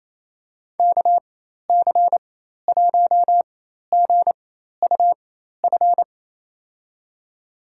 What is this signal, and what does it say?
Morse code, heard as a keyed tone, about 27 wpm, spaced as for 13 wpm, 700 Hz. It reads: KC1GUF
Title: USA Call Signs - 6 Characters - 4x Character Spacing - 28wpm